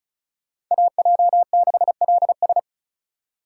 Morse code, heard as a keyed tone, about 46 wpm, 700 Hz. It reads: AJ6LS